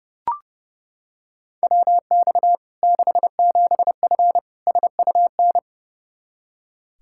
Morse code, heard as a keyed tone, about 32 wpm, 700 Hz. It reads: WX 67F SUN
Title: K1USN SST QSO Practice 30wpm